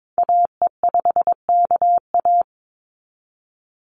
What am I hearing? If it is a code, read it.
Morse code, heard as a keyed tone, about 22 wpm, 700 Hz. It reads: AE5KA
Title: NA SOTA Activator Callsigns 22wpm